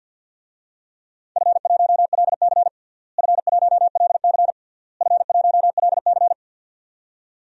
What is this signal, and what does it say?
Morse code, heard as a keyed tone, about 52 wpm, 700 Hz. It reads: F1LC F1LC F1LC
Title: DXCC-100 Call Signs - 4 Characters - Repeated 3x - 50wpm